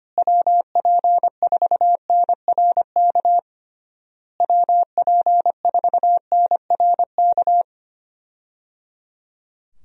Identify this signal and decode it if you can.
Morse code, heard as a keyed tone, about 25 wpm, 700 Hz. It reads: WP4NRK WP4NRK
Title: USA Call Signs - 6 Characters - Repeated 3x - 25wpm